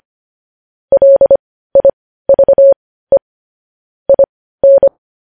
Morse code, heard as a keyed tone, about 25 wpm, spaced as for 15 wpm, 550 Hz. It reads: LIVE IN
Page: news headlines in morse code at 15 wpm